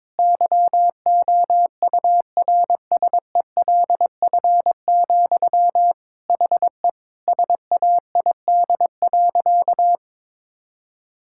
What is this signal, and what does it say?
Morse code, heard as a keyed tone, about 22 wpm, 700 Hz. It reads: YOURSELF, HE SAID.
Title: Sentences from Top 1600 Words 22wpm